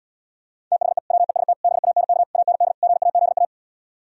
Morse code, heard as a keyed tone, about 77 wpm, 700 Hz. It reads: THE WEST BANK AND GAZA